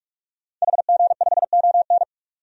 Morse code, heard as a keyed tone, about 44 wpm, 700 Hz. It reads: HG5ON
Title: DXCC-100 Call Signs - 5 Characters - 45wpm